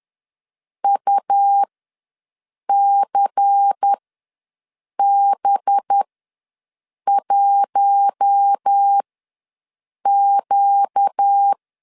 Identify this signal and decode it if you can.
Morse code, heard as a keyed tone, about 10 wpm, spaced as for 6 wpm, 800 Hz. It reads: UCB1Q
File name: LEZIONI / 7 GF9Z / 7 GF9 ws 10wpm 4034